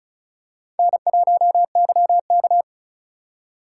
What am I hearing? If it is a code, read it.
Morse code, heard as a keyed tone, about 36 wpm, 700 Hz. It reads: N1YK
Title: USA Call Signs - 4 Characters - 2x Character Spacing - 35wpm